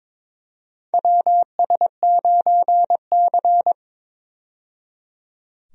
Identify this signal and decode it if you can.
Morse code, heard as a keyed tone, about 22 wpm, 700 Hz. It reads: WS9C
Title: USA Call Signs - 4 Characters - 22wpm